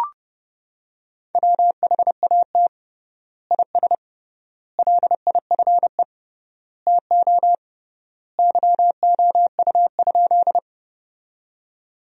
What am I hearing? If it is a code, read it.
Morse code, heard as a keyed tone, about 30 wpm, 700 Hz. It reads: WHAT IS LIFE TO YOU?
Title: 3x Word-Spacing - Sentences from Top 300 Words - 30wpm